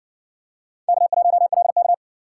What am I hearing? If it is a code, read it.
Morse code, heard as a keyed tone, about 56 wpm, 700 Hz. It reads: K1LK